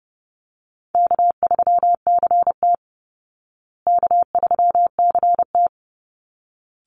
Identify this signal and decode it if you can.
Morse code, heard as a keyed tone, about 30 wpm, 700 Hz. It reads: K3CT K3CT
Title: Call Signs Repeated 3x - 30wpm